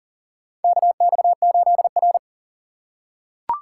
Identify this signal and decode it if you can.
Morse code, heard as a keyed tone, about 40 wpm, 700 Hz. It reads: KX8R
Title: USA Call Signs - 4 Characters - 4x Character Spacing - 40wpm